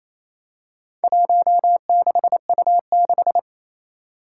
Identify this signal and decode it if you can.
Morse code, heard as a keyed tone, about 28 wpm, 700 Hz. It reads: 16U6
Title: Four Challenging Letter-Number Combinations 28wpm